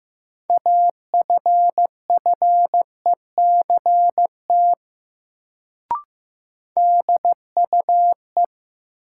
Morse code, heard as a keyed tone, about 15 wpm, 700 Hz. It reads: AFFECT DUE
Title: Sentences from Top 900 Words - New Word Drill - 15wpm